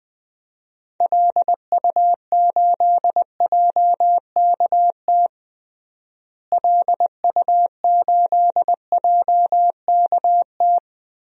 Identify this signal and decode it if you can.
Morse code, heard as a keyed tone, about 20 wpm, 700 Hz. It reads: LU8JKT LU8JKT
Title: DXCC-100 Call Signs - 6 Characters - Repeated 3x - 20wpm